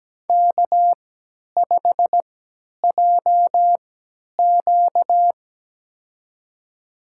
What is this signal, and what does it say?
Morse code, heard as a keyed tone, about 17 wpm, spaced as for 10 wpm, 700 Hz. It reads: K5JQ
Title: USA Call Signs - 4 Characters - 3x Character Spacing - 17wpm